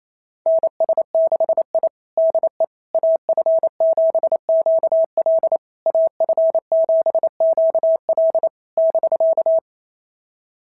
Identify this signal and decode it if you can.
Morse code, heard as a keyed tone, about 27 wpm, 650 Hz. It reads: NS6I DE AF7QL AF7QL <BK>
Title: SOTA QSO Practice 28wpm